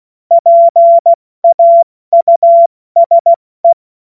Morse code, heard as a keyed tone, about 16 wpm, 680 Hz. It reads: PAUSE
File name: Morsekurs / Witze / Witz 01. August 2023 70 F80